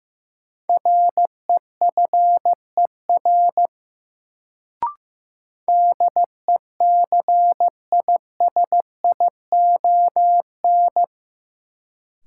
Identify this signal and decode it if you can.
Morse code, heard as a keyed tone, about 15 wpm, 700 Hz. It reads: REFER DECISION
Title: Sentences from Top 1000 Words - New Word Drill - 15wpm